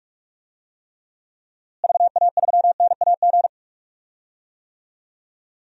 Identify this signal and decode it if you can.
Morse code, heard as a keyed tone, about 46 wpm, 700 Hz. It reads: VA3NAG
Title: DXCC-100 Call Signs - 6 Characters - Repeated 3x - 45wpm